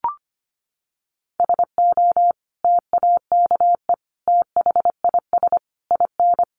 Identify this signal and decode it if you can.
Morse code, heard as a keyed tone, about 25 wpm, 700 Hz. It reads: SO TAKE THIS IN
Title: Sentences from Top 1500 Words 25wpm